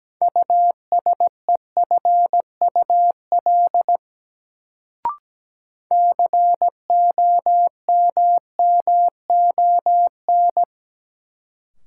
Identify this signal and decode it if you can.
Morse code, heard as a keyed tone, about 17 wpm, 700 Hz. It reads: USEFUL COMMON
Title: Sentences from Top 700 Words - New Word Drill - 17wpm